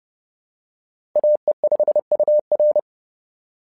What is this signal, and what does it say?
Morse code, heard as a keyed tone, about 30 wpm, 600 Hz. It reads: AE5UR
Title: SOTA QSO Practice 30wpm